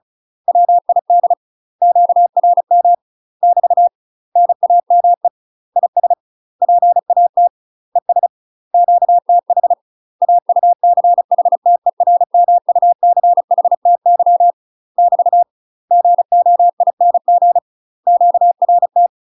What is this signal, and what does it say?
Morse code, heard as a keyed tone, about 35 wpm, 700 Hz. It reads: WID QRM <BT> NAME IS PAT ES QTH AUCHTERMUCHTY <BT> GOING QRT
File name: UK MASTER QSOs TRAINING/35WPM